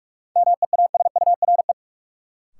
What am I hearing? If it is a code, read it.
Morse code, heard as a keyed tone, about 42 wpm, 700 Hz. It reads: MEASURE
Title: Sentences from Top 1000 Words - Review Part 1 - New Word Drill - 45wpm